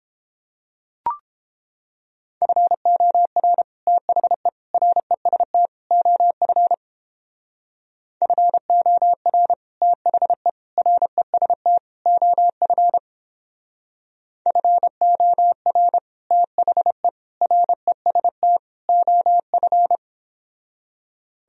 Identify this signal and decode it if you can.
Morse code, heard as a keyed tone, about 29 wpm, 700 Hz. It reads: FOR THE REST OF FOR THE REST OF FOR THE REST OF